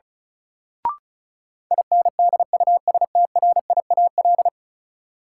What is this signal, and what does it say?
Morse code, heard as a keyed tone, about 36 wpm, 700 Hz. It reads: INDUSTRIAL
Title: Sets of 5 Words - New Word Drill - 35wpm